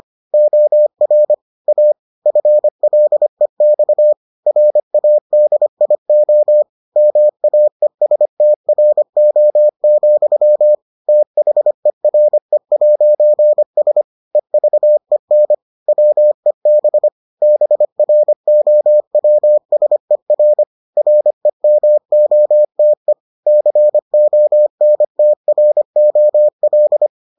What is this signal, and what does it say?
Morse code, heard as a keyed tone, about 25 wpm, 600 Hz. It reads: OR A FLEX RADIO MAESTRO, THERE'S EVEN WEB BROWSER REMOTE CONTROL